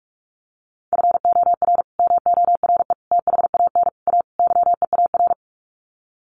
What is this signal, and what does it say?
Morse code, heard as a keyed tone, about 44 wpm, 700 Hz. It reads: FOR MORE THAN A YEAR
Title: Sets of 5 Words 45wpm